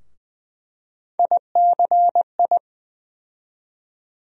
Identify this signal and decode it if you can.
Morse code, heard as a keyed tone, about 21 wpm, 700 Hz. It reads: ICI
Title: USA Call Sign Postfixes - 3 Letters 20wpm